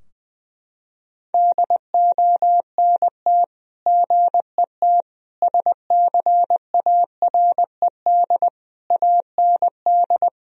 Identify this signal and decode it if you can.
Morse code, heard as a keyed tone, about 21 wpm, 700 Hz. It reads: DONT GET SCARED AND
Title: Sentences from Top 1000 Words - Review Part 4 - 20wpm